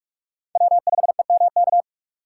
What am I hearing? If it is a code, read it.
Morse code, heard as a keyed tone, about 43 wpm, 700 Hz. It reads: W5EMK
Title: USA Call Signs - 5 Characters - 2x Character Spacing - 45wpm